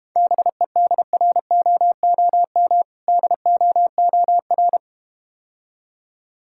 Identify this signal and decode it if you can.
Morse code, heard as a keyed tone, about 33 wpm, 700 Hz. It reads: BEDROOM DOOR